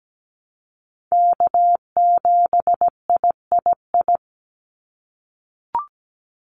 Morse code, heard as a keyed tone, about 17 wpm, 700 Hz. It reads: K7III
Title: NA SOTA Chaser Callsigns 17wpm